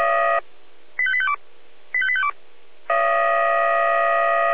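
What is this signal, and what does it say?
Five-tone selective call (ZVEI), heard as quick runs of five tones, tones of about 70 ms. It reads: 85852 85852
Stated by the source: ZVEI five-tone call received by SDR.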